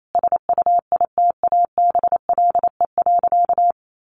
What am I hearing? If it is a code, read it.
Morse code, heard as a keyed tone, about 28 wpm, 700 Hz. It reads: SUITABLE.
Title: Sentences from Top 1400 Words 28wpm